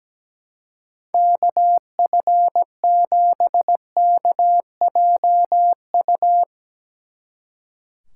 Morse code, heard as a keyed tone, about 18 wpm, 700 Hz. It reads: KF7KJU
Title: Speed Racing - USA Call Signs 17wpm